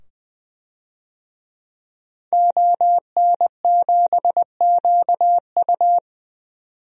Morse code, heard as a keyed tone, about 20 wpm, 700 Hz. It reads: ON7QU
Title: DXCC-100 Call Signs - 5 Characters - 20wpm